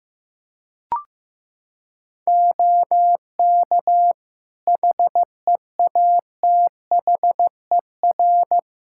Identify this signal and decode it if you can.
Morse code, heard as a keyed tone, about 15 wpm, 700 Hz. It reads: OK HEATHER
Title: K1USN SST QSO Practice 15wpm